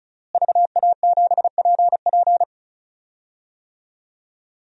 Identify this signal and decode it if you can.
Morse code, heard as a keyed tone, about 36 wpm, 700 Hz. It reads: VA7PP
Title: DXCC-100 Call Signs - 5 Characters - Repeated 3x - 35wpm